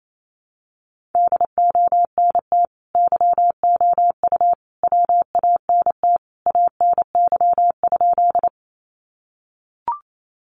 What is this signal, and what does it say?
Morse code, heard as a keyed tone, about 28 wpm, 700 Hz. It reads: DONT YOU WANT ANY?
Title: Sentences from Top 300 Words 28wpm